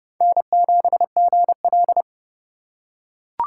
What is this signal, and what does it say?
Morse code, heard as a keyed tone, about 30 wpm, 700 Hz. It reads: N7GL